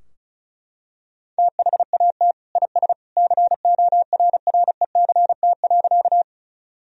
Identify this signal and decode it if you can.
Morse code, heard as a keyed tone, about 34 wpm, 700 Hz. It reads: THAT IS CORRECT.
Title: Sentences from Top 1000 Words - Review Part 2 - 35wpm